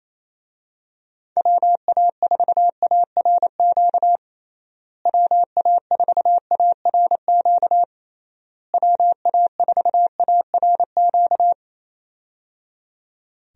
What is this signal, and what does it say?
Morse code, heard as a keyed tone, about 28 wpm, 700 Hz. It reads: WA4ARQ WA4ARQ WA4ARQ
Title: USA Call Signs - 6 Characters - Repeated 3x - 28wpm